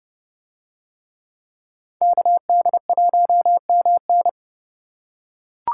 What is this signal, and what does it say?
Morse code, heard as a keyed tone, about 30 wpm, 700 Hz. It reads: KD1MN